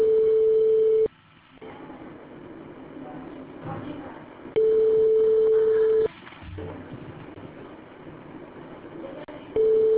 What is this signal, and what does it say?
Ringback tone (France), 440 Hz.